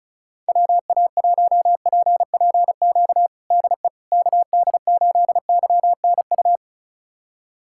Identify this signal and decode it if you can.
Morse code, heard as a keyed tone, about 34 wpm, 700 Hz. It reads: WA1PPQ DE KD8YNU
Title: USA Callsign DE Callsign - 6 Characters - 35wpm